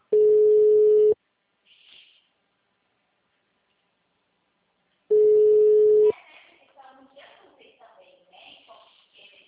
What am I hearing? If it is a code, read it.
Ringback tone (Europe, 425 Hz), with speech.